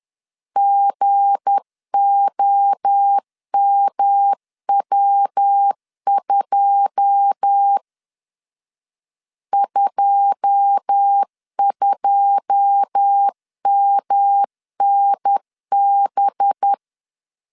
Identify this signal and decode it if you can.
Morse code, heard as a keyed tone, about 10 wpm, 800 Hz. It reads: GOMW2 22MNB
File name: LEZIONI / 9 2 10wpm 2736